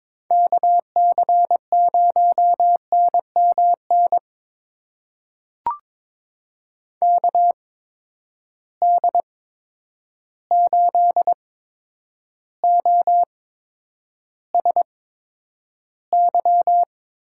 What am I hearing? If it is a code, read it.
Morse code, heard as a keyed tone, about 22 wpm, spaced as for 9 wpm, 700 Hz. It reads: KC0NMN KD8OSY
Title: USA Call Signs - 6 Characters - 8x Character Spacing - 22wpm